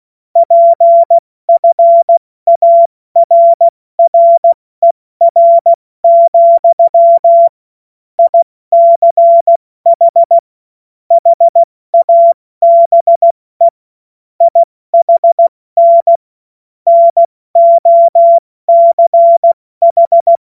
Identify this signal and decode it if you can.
Morse code, heard as a keyed tone, about 16 wpm, 680 Hz. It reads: PFARRER, ICH HABE IHN NOCH
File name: Morsekurs / Witze / Witz 31. Mai 2022 70 F80